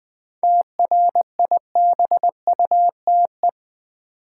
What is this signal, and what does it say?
Morse code, heard as a keyed tone, about 20 wpm, 700 Hz. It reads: TRIBUTE
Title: 7 Letter Words - First 3 Letters Given - 20wpm